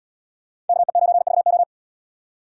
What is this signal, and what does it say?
Morse code, heard as a keyed tone, about 77 wpm, 690 Hz. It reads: KE9VO